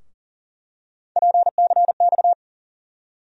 Morse code, heard as a keyed tone, about 38 wpm, 700 Hz. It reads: PCX
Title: USA Call Sign Postfixes - 3 Letters 40wpm